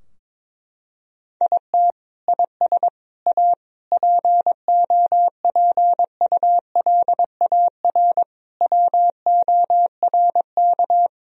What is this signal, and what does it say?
Morse code, heard as a keyed tone, about 21 wpm, 700 Hz. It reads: IT IS A POPULAR WORK
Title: Sentences from Top 1000 Words - Review Part 3 - 22wpm